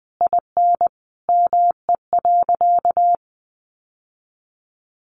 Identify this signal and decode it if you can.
Morse code, heard as a keyed tone, about 20 wpm, 700 Hz. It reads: IN ME.
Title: Sentences from Top 600 Words 20wpm